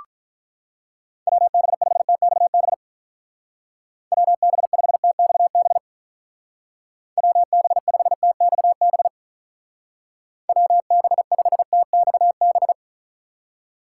Morse code, heard as a keyed tone, about 46 wpm, 700 Hz. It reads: WB5TXB WB5TXB WB5TXB WB5TXB